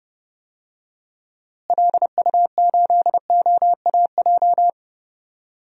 Morse code, heard as a keyed tone, about 30 wpm, 700 Hz. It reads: LU8OAJ
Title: DXCC-100 Call Signs - 6 Characters - Repeated 2x - 30wpm